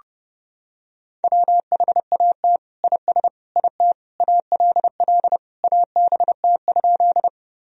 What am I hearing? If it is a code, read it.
Morse code, heard as a keyed tone, about 30 wpm, 700 Hz. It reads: WHAT IS IT ALL ABT?
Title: Sentences from Top 100 Words and Abbreviations 30wpm